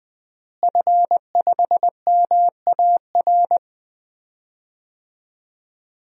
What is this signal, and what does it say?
Morse code, heard as a keyed tone, about 21 wpm, 700 Hz. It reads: F5MAR
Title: DXCC-100 Call Signs - 5 Characters - Repeated 2x - 20wpm